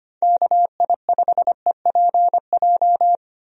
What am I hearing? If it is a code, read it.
Morse code, heard as a keyed tone, about 25 wpm, 700 Hz. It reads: KI5EPJ
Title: USA Call Signs - 6 Characters - Repeated 2x - 25wpm